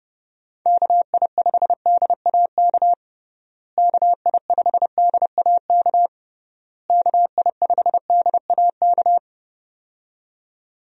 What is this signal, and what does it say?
Morse code, heard as a keyed tone, about 32 wpm, 700 Hz. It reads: KI5DAK KI5DAK KI5DAK
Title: USA Call Signs - 6 Characters - Repeated 3x - 30wpm